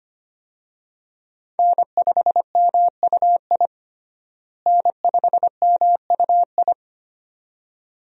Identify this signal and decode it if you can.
Morse code, heard as a keyed tone, about 25 wpm, 700 Hz. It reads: N5MUI N5MUI
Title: USA Call Signs - 5 Characters - Repeated 2x - 25wpm